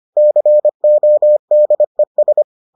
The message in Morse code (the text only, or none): CODES